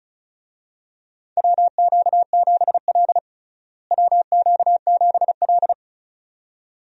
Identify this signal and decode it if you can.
Morse code, heard as a keyed tone, about 36 wpm, 700 Hz. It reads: WQ7L WQ7L
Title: USA Call Signs - 4 Characters - Repeated 2x - 35wpm